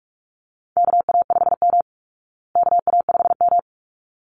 Morse code, heard as a keyed tone, about 44 wpm, 700 Hz. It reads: KA5M KA5M
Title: Call Signs Repeated 3x - 45wpm